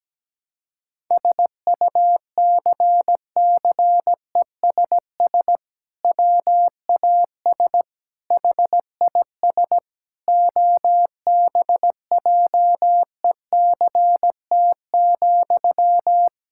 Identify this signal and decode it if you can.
Morse code, heard as a keyed tone, about 17 wpm, 700 Hz. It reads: SUCCESS WAS HIS OBJECT,
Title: Sentences from Top 1000 Words - Review Part 6 - 17wpm